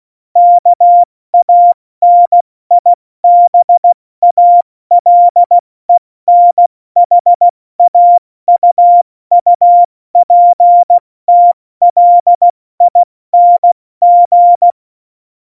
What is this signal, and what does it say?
Morse code, heard as a keyed tone, about 16 wpm, 700 Hz. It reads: KANIBALENHAUUPTLING
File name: Morsekurs / Witze / Witz 12. Januar 2021, 70 F80